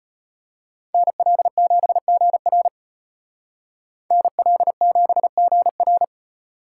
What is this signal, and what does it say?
Morse code, heard as a keyed tone, about 31 wpm, 700 Hz. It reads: NL7GR NL7GR
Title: Speed Racing - USA 2x2 Call Signs 25wpm